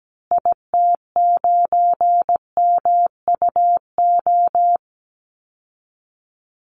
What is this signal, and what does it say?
Morse code, heard as a keyed tone, about 17 wpm, 700 Hz. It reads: IT9MUO
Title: Call Signs Repeated 2x - 17wpm